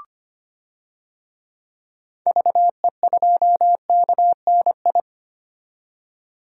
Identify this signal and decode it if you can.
Morse code, heard as a keyed tone, about 25 wpm, 700 Hz. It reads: VE2KNI
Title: DXCC-100 Call Signs - 6 Characters - 25wpm